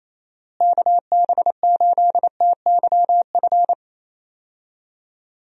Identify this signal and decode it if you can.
Morse code, heard as a keyed tone, about 27 wpm, 700 Hz. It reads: KB8TYF
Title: USA Call Signs - 6 Characters - Repeated 2x - 28wpm